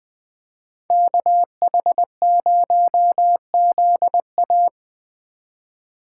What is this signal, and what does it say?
Morse code, heard as a keyed tone, about 20 wpm, 700 Hz. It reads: KH0ZA
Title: Speed Racing - USA 2x2 Call Signs 17wpm